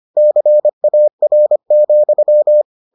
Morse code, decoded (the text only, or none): CAR,